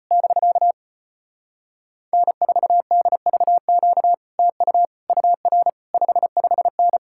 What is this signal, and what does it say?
Morse code, heard as a keyed tone, about 33 wpm, 700 Hz. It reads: <BK> N4DVQ TU UR 55N